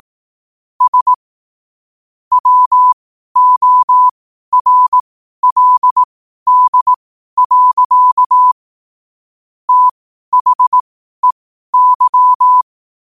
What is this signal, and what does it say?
Morse code, heard as a keyed tone, about 18 wpm, spaced as for 12 wpm, 1000 Hz. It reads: S WORLD. THEY